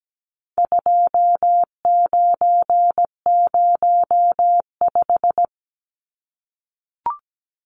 Morse code, heard as a keyed tone, about 17 wpm, 700 Hz. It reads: 2905
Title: Numbers - 4-Digit 17wpm